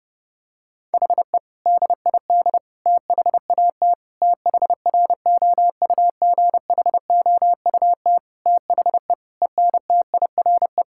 Morse code, heard as a keyed tone, about 30 wpm, 700 Hz. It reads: HE DID THAT THROUGHOUT THE ENTIRE
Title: Sentences from Top 1000 Words - Review Part 5 - 30wpm